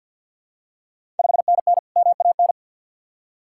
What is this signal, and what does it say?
Morse code, heard as a keyed tone, about 53 wpm, 700 Hz. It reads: 5NN MAN
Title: Ontario QSO Party Exchange 50wpm